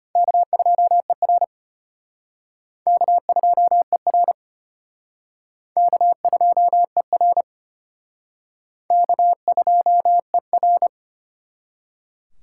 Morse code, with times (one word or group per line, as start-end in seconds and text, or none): 0.2-1.4 K2ER
2.9-4.3 K2ER
5.8-7.4 K2ER
8.9-10.9 K2ER